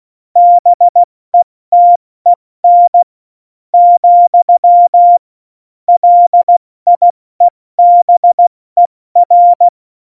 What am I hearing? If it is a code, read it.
Morse code, heard as a keyed tone, about 16 wpm, spaced as for 14 wpm, 700 Hz. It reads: BETEN , LIEBER
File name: Morsekurs / Witze / Witz 09. Oktober 2018, 70 F80